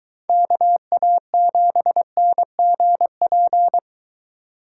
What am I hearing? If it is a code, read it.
Morse code, heard as a keyed tone, about 23 wpm, 700 Hz. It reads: KA7NGP